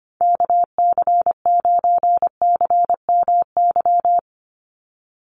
Morse code, heard as a keyed tone, about 25 wpm, 700 Hz. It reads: KC9CMY
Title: USA Call Signs 25wpm